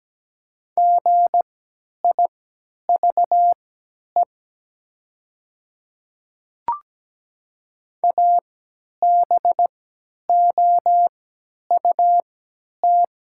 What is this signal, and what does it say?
Morse code, heard as a keyed tone, about 17 wpm, spaced as for 10 wpm, 700 Hz. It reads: GIVE ABOUT
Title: 3x Character Spacing - Top 300 Words - 17wpm